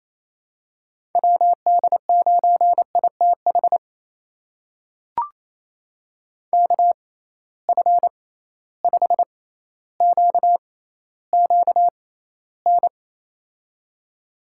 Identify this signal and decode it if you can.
Morse code, heard as a keyed tone, about 28 wpm, spaced as for 14 wpm, 700 Hz. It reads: WD9ITH KF5QQN
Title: USA Call Signs - 6 Characters - 6x Character Spacing - 28wpm